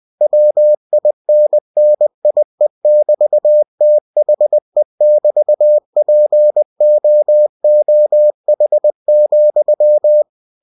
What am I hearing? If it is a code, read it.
Morse code, heard as a keyed tone, about 20 wpm, 600 Hz. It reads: WINNIE<BT>THE<BT>POOH,
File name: cw-books / winnie-the-pooh / winnie-20wpm-page-0028